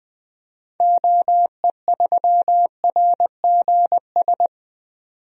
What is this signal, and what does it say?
Morse code, heard as a keyed tone, about 20 wpm, 700 Hz. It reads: OE3RGS
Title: DXCC-100 Call Signs - 6 Characters - 20wpm